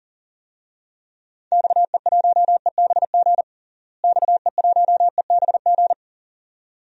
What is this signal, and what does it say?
Morse code, heard as a keyed tone, about 37 wpm, 700 Hz. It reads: XE1EBG XE1EBG
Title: DXCC-100 Call Signs - 6 Characters - Repeated 2x - 40wpm